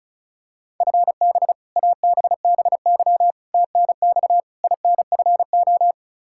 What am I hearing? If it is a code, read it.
Morse code, heard as a keyed tone, about 34 wpm, 700 Hz. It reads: FB ABBY TNX INFO
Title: Practice QSO - Exchange 3 - More Info - 35wpm